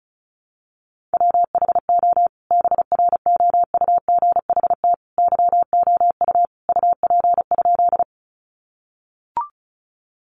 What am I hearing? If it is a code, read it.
Morse code, heard as a keyed tone, about 34 wpm, 700 Hz. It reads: WHO BROUGHT YOU UP?
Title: Sentences from Top 500 Words 35wpm